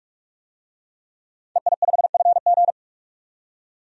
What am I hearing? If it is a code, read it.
Morse code, heard as a keyed tone, about 41 wpm, 700 Hz. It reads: EI5FG
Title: DXCC-100 Call Signs - 5 Characters - 45wpm